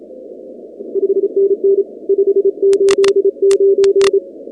Morse code, heard as a keyed tone, about 25 wpm, 390 Hz. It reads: 5NN 589
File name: contests / 2019-cw-wpx-kt5j-k5pi- -2330 0-8